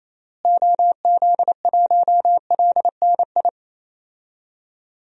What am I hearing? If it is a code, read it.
Morse code, heard as a keyed tone, about 27 wpm, 700 Hz. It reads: OZ1LNI
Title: DXCC-100 Call Signs - 6 Characters - Repeated 3x - 28wpm